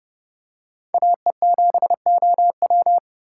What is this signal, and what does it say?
Morse code, heard as a keyed tone, about 30 wpm, 700 Hz. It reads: AE7OW
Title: USA Call Signs - 5 Characters - 6x Character Spacing - 30wpm